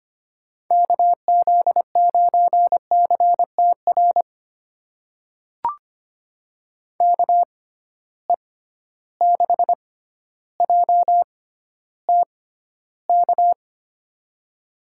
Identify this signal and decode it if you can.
Morse code, heard as a keyed tone, about 25 wpm, spaced as for 13 wpm, 700 Hz. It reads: KZ9CTR KE6JTK
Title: USA Call Signs - 6x Character Spacing - 25wpm